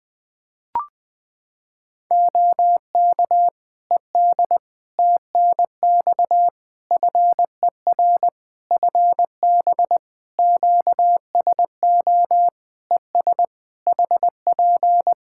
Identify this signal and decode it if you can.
Morse code, heard as a keyed tone, about 21 wpm, 700 Hz. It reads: OK ED TNX FER FB QSO ES HP
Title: K1USN SST QSO Practice 20wpm